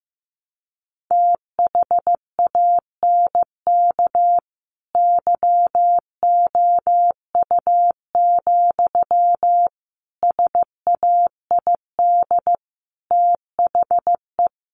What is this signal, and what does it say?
Morse code, heard as a keyed tone, about 15 wpm, 700 Hz. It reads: THANK YOU, SAID THE
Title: Sentences from Top 1100 Words 15wpm